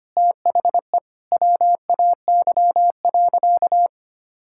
Morse code, heard as a keyed tone, about 25 wpm, 700 Hz. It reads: THE WAY.
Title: Sentences from Top 1000 Words - Review Part 5 - 25wpm